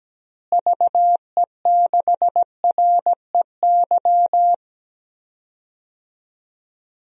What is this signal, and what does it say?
Morse code, heard as a keyed tone, about 17 wpm, 700 Hz. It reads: VE6REY
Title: DXCC-100 Call Signs - 6 Characters - Repeated 3x - 17wpm